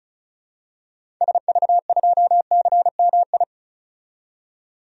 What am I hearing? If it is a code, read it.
Morse code, heard as a keyed tone, about 34 wpm, 700 Hz. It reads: SV2CMI
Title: DXCC-100 Call Signs - 6 Characters - 35wpm